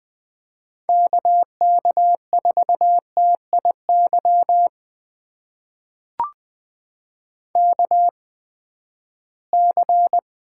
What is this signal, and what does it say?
Morse code, heard as a keyed tone, about 20 wpm, spaced as for 14 wpm, 700 Hz. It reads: KK4TIY KC
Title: USA Call Signs - 6 Characters - 8x Character Spacing - 20wpm